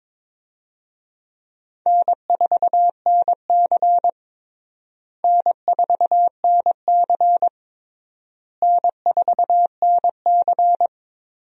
Morse code, heard as a keyed tone, about 22 wpm, 700 Hz. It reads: N4NC N4NC N4NC
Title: USA Call Signs - 4 Characters - Repeated 3x - 22wpm